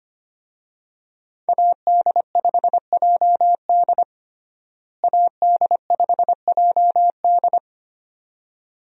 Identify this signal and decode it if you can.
Morse code, heard as a keyed tone, about 25 wpm, 700 Hz. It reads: AD5JD AD5JD
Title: USA Call Signs - 5 Characters - Repeated 2x - 25wpm